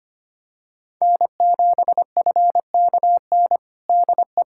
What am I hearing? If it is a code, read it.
Morse code, heard as a keyed tone, about 25 wpm, 700 Hz. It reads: N7FKN DE
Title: Practice QSO - Exchange 4 - Ending - 25wpm